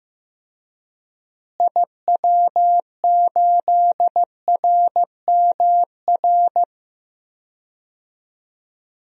Morse code, heard as a keyed tone, about 15 wpm, 700 Hz. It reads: IW8RMR
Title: DXCC-100 Call Signs - 6 Characters - Repeated 3x - 15wpm